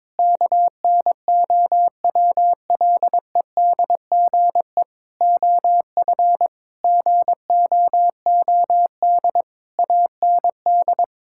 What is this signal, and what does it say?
Morse code, heard as a keyed tone, about 22 wpm, 700 Hz. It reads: KNOWLEDGE OF GOOD AND
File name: Sets of 9 Words - 22wpm